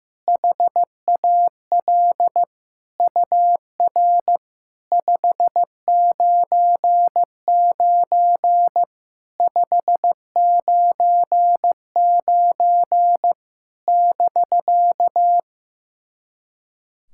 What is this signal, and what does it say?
Morse code, heard as a keyed tone, about 15 wpm, 700 Hz. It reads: HAL UR 599 599 <BK>
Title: SOTA QSO Practice 15wpm